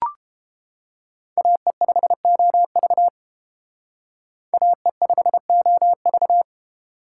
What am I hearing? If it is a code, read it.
Morse code, heard as a keyed tone, about 28 wpm, 700 Hz. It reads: AE5OV AE5OV